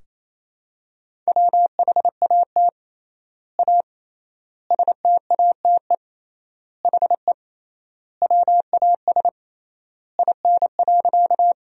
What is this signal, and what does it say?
Morse code, heard as a keyed tone, about 28 wpm, 700 Hz. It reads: WHAT A STATE HE WAS IN.
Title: 3x Word-Spacing - Sentences from Top 400 Words - 28wpm